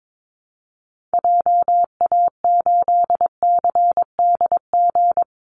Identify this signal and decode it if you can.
Morse code, heard as a keyed tone, about 22 wpm, 700 Hz. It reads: JA8CDG
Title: Call Signs Repeated 2x - 22wpm